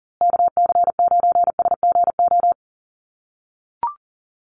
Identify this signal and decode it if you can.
Morse code, heard as a keyed tone, about 40 wpm, 700 Hz. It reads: KC9SGO
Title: USA Call Signs Repeated 3x 40wpm